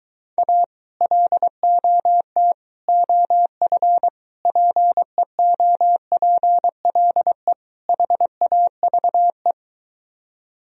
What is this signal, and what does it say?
Morse code, heard as a keyed tone, about 22 wpm, 700 Hz. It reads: A LOT OF PEOPLE HAVE